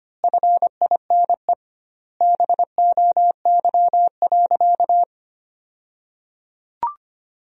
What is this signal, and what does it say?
Morse code, heard as a keyed tone, about 25 wpm, 700 Hz. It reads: FINE BOY.
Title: 2x Word-Spacing - Sentences from Top 500 Words - 25wpm